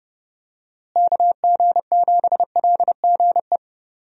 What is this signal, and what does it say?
Morse code, heard as a keyed tone, about 30 wpm, 700 Hz. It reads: KG7LGE
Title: USA Call Signs - 6 Characters - Repeated 3x - 30wpm